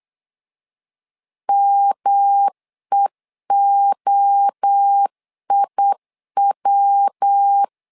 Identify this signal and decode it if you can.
Morse code, heard as a keyed tone, about 8 wpm, 800 Hz. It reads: MEOIW
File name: LEZIONI / 4 Q1 8wpm 3035